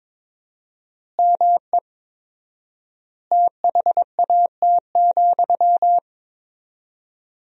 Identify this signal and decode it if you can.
Morse code, heard as a keyed tone, about 22 wpm, 700 Hz. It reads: ME THAT,
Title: 4x Word-Spacing - Sentences from Top 500 Words - 22wpm